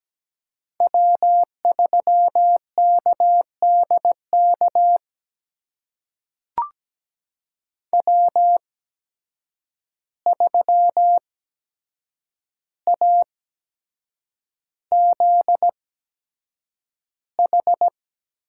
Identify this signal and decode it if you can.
Morse code, heard as a keyed tone, about 17 wpm, spaced as for 7 wpm, 700 Hz. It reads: W3KDK W3AZH
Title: USA Call Signs - 5 Characters - 8x Character Spacing - 17wpm